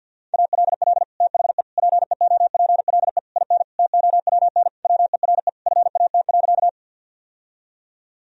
Morse code, heard as a keyed tone, about 52 wpm, 700 Hz. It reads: ALL THE PEOPLE IN TOWN WERE FAT.
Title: Sentences from Top 1000 Words - Review Part 4 - 50wpm